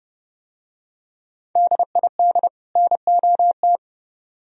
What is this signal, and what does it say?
Morse code, heard as a keyed tone, about 28 wpm, 700 Hz. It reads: DID NOT